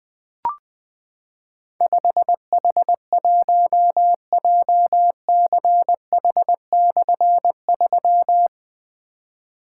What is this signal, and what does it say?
Morse code, heard as a keyed tone, about 20 wpm, 700 Hz. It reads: 5H1JCH/3
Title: Difficult Call Signs 20wpm